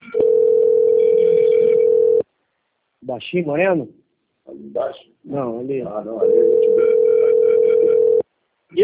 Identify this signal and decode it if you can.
Ringback tone (North America), 440 + 480 Hz, with speech.
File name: rg-1600-unknown- - - .12713